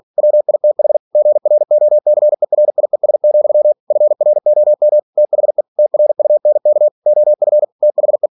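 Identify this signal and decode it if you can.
Morse code, heard as a keyed tone, about 46 wpm, 600 Hz. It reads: WITH GROCERIES, FROM THE TRUNK OF THE
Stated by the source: Practice text from an English novel sent at 47 WPM.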